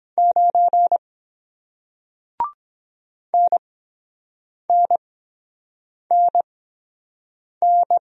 Morse code, heard as a keyed tone, about 22 wpm, spaced as for 7 wpm, 700 Hz. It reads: 9 NNNN